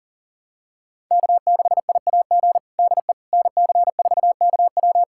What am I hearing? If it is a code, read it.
Morse code, heard as a keyed tone, about 40 wpm, 700 Hz. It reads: K6IAG DE NC4KW
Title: USA Callsign DE Callsign - 5 Characters - 40wpm